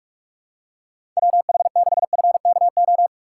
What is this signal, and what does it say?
Morse code, heard as a keyed tone, about 46 wpm, 700 Hz. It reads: WH6FKO
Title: USA Call Signs - 6 Characters - 2x Character Spacing - 45wpm